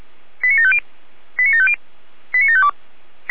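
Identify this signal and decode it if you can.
Five-tone selective call (ZVEI), heard as quick runs of five tones, tones of about 80 ms. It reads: 89750 89750 89752
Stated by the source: ZVEI five-tone call 89436 received by SDR.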